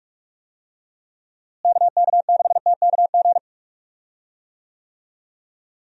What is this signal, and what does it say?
Morse code, heard as a keyed tone, about 46 wpm, 700 Hz. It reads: KK6TKG
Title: USA Call Signs - 6 Characters - Repeated 3x - 45wpm